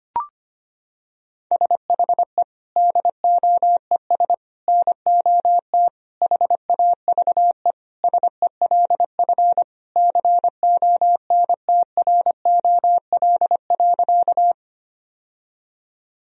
Morse code, heard as a keyed tone, about 25 wpm, 700 Hz. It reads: SHE DOES NOT HAVE SELF CONTROL.
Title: Sentences from Top 1000 Words - Review Part 3 - 25wpm